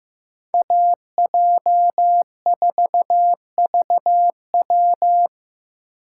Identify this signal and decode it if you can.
Morse code, heard as a keyed tone, about 18 wpm, 700 Hz. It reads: AJ4VW
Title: Speed Racing - USA 2x2 Call Signs 15wpm